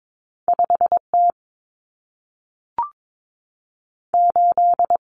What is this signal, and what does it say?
Morse code, heard as a keyed tone, about 22 wpm, 700 Hz. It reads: 5T 8
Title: Common Cut Numbers - 2-Digit 22wpm